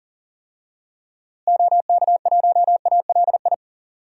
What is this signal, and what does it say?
Morse code, heard as a keyed tone, about 37 wpm, 700 Hz. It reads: OK1ALI
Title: DXCC-100 Call Signs - 6 Characters - Repeated 3x - 40wpm